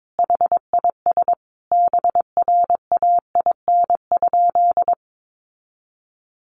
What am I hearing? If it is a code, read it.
Morse code, heard as a keyed tone, about 22 wpm, 700 Hz. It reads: HIS BRAIN?
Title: Sentences from Top 1100 Words 22wpm